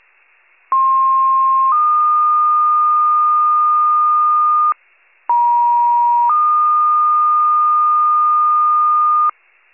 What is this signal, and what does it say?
Two-tone page, heard as a one-second tone, then a three-second tone, about 1060 then 1220 Hz, about 950 then 1220 Hz. It is heard from 0.7 to 9.3 s.